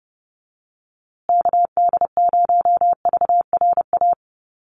Morse code, heard as a keyed tone, about 30 wpm, 700 Hz. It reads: KD0VRA
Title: USA Call Signs Repeated 2x 30wpm